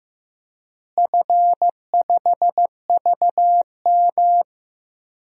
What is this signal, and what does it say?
Morse code, heard as a keyed tone, about 16 wpm, 700 Hz. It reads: F5VM
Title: DXCC-100 Call Signs - 4 Characters - Repeated 3x - 15wpm